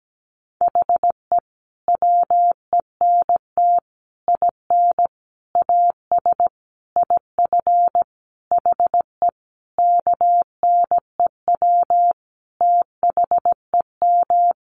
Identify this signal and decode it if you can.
Morse code, heard as a keyed tone, about 17 wpm, 700 Hz. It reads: HE WENT IN AS IF HE KNEW THEM
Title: Sentences from Top 1400 Words 17wpm